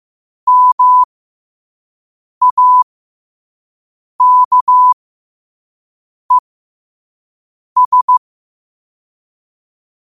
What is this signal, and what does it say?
Morse code, heard as a keyed tone, about 15 wpm, spaced as for 5 wpm, 1000 Hz. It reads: MAKES
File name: qotdfiles / 5wpm